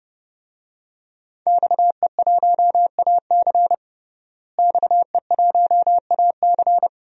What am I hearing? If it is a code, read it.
Morse code, heard as a keyed tone, about 32 wpm, 700 Hz. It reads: XE1AC XE1AC
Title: DXCC-100 Call Signs - 5 Characters - Repeated 3x - 30wpm